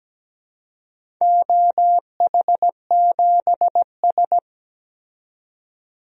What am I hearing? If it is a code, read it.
Morse code, heard as a keyed tone, about 17 wpm, 700 Hz. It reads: OH7S
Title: DXCC-100 Call Signs - 4 Characters - 17wpm